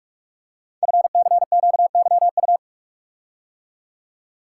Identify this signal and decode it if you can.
Morse code, heard as a keyed tone, about 44 wpm, 700 Hz. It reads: FCQYU
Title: Code Groups - 5 Letters - 45wpm